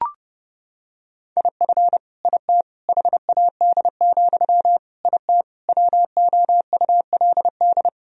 Morse code, heard as a keyed tone, about 30 wpm, 700 Hz. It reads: IF IT HAD, IT WOULD
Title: Sentences from Top 1000 Words - Review Part 3 - 30wpm